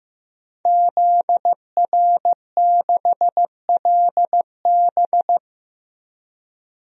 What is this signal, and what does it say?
Morse code, heard as a keyed tone, about 15 wpm, 700 Hz. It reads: ZR6LB
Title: DXCC-100 Call Signs - 5 Characters - 15wpm